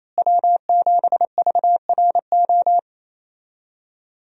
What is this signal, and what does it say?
Morse code, heard as a keyed tone, about 28 wpm, 700 Hz. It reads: W7VRO
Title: USA Call Signs - 5 Characters - Repeated 3x - 28wpm